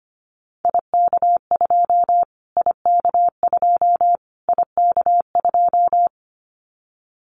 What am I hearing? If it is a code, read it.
Morse code, heard as a keyed tone, about 25 wpm, 700 Hz. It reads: IK2 IK2 IK2
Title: Callsign Prefixes for Top 25 DXCC - Repeated 3x - 25wpm